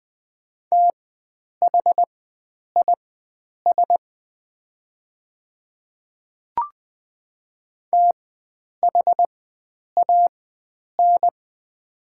Morse code, heard as a keyed tone, about 20 wpm, spaced as for 9 wpm, 700 Hz. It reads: THIS THAN
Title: 4x Character Spacing - Top 100 Words - 20wpm